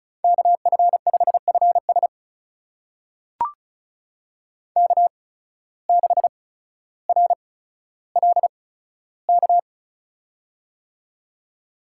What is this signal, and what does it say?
Morse code, heard as a keyed tone, about 36 wpm, spaced as for 15 wpm, 700 Hz. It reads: KF5FS K6RLK
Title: USA Call Signs - 5 Characters - 8x Character Spacing - 35wpm